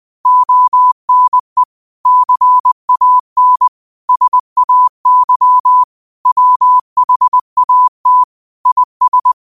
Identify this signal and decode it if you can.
Morse code, heard as a keyed tone, about 20 wpm, 1000 Hz. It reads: ONE CAN SAY WHAT IS
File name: qotdfiles / 20wpm